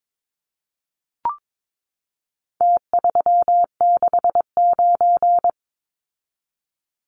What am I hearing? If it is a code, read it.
Morse code, heard as a keyed tone, about 22 wpm, 700 Hz. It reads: T369
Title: Numbers - Contest Style - 4-Digit 22wpm